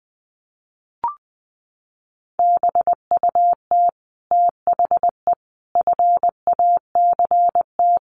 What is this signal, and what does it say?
Morse code, heard as a keyed tone, about 20 wpm, 700 Hz. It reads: BUT THE FACT
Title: Sets of 5 Words Encore2 20wpm